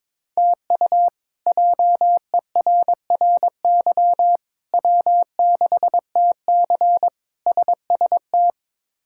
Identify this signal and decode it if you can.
Morse code, heard as a keyed tone, about 22 wpm, 700 Hz. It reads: TU JERRY W6TC SST
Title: K1USN SST QSO Practice - 22wpm